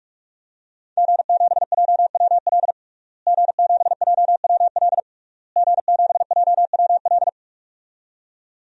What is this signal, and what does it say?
Morse code, heard as a keyed tone, about 44 wpm, 700 Hz. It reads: G7JWL G7JWL G7JWL
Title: DXCC-100 Call Signs - 5 Characters - Repeated 3x - 45wpm